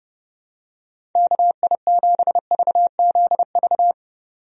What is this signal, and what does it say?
Morse code, heard as a keyed tone, about 30 wpm, 700 Hz. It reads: KI7VZV
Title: USA Call Signs - 6 Characters - 30wpm